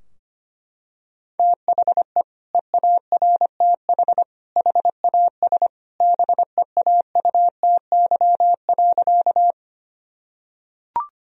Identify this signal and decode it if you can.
Morse code, heard as a keyed tone, about 25 wpm, 700 Hz. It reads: THE EARTH HAS BEAUTY.
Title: Sentences from Top 1000 Words - Review Part 2 - 25wpm